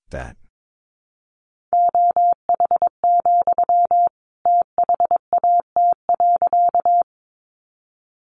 Morse code, heard as a keyed tone, about 22 wpm, 700 Hz. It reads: OH, THAT.
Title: Sentences from Top 500 Words 22wpm